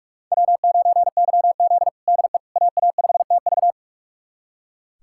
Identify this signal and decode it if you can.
Morse code, heard as a keyed tone, about 46 wpm, 700 Hz. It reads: W9YZ DE AA5TV
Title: USA Callsign DE Callsign 45wpm